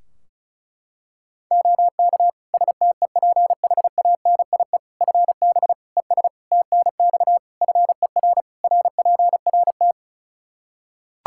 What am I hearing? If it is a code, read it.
Morse code, heard as a keyed tone, about 34 wpm, 700 Hz. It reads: OK STEPHANIE FB ES TNX FER RPRT
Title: Practice QSO - Exchange 2 - Info - 35wpm